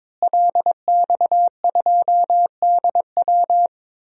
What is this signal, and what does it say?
Morse code, heard as a keyed tone, about 21 wpm, 700 Hz. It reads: LX2DW
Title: DXCC-100 Call Signs - 5 Characters - Repeated 2x - 22wpm